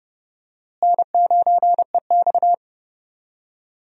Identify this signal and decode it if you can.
Morse code, heard as a keyed tone, about 30 wpm, 700 Hz. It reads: N9EX
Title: USA Call Signs - 4 Characters - Repeated 3x - 30wpm